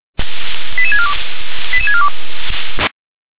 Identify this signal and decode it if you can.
Five-tone selective call (ZVEI), heard as quick runs of five tones, tones of about 70 ms. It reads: 88642 88642